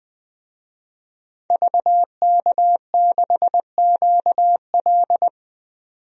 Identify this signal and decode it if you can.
Morse code, heard as a keyed tone, about 21 wpm, 700 Hz. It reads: VK6QL
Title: DXCC-100 Call Signs - 5 Characters - 20wpm